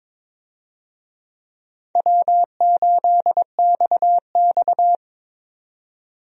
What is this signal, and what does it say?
Morse code, heard as a keyed tone, about 21 wpm, 700 Hz. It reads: W8XX
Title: USA Call Signs - 4 Characters - 22wpm